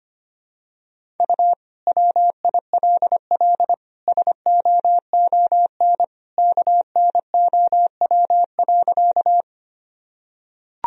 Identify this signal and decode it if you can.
Morse code, heard as a keyed tone, about 25 wpm, 700 Hz. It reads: U WILL SOON KNOW.
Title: Sentences from Top 400 Words and Abbreviations 25wpm